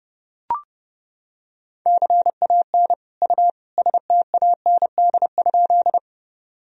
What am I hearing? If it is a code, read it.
Morse code, heard as a keyed tone, about 30 wpm, 700 Hz. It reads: CAN U STAND?
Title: Sentences from Top 300 Words and Abbreviations 30wpm